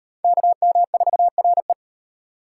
Morse code, heard as a keyed tone, about 34 wpm, 700 Hz. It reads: KM4RE